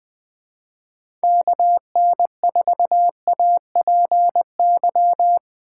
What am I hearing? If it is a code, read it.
Morse code, heard as a keyed tone, about 21 wpm, 700 Hz. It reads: KN4APY
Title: USA Call Signs - 6 Characters - 3x Character Spacing - 20wpm